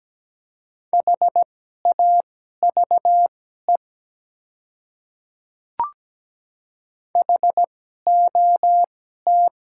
Morse code, heard as a keyed tone, about 17 wpm, spaced as for 12 wpm, 700 Hz. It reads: HAVE HOT
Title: 2x Character Spacing - Top 100 Words - 17wpm